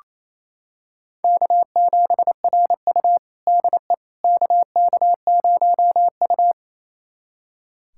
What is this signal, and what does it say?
Morse code, heard as a keyed tone, about 28 wpm, 700 Hz. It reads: K7RU DE KK0U
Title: USA Callsign DE Callsign - 4 Characters - 28wpm